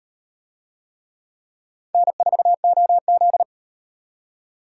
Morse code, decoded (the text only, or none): N4OZ